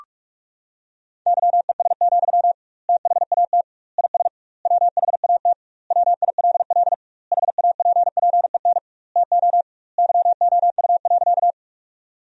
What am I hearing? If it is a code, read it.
Morse code, heard as a keyed tone, about 47 wpm, 700 Hz. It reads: YES, THAT IS WHAT WILL HAPPEN TO YOU.
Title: 2x Word-Spacing - Sentences from Top 500 Words - 45wpm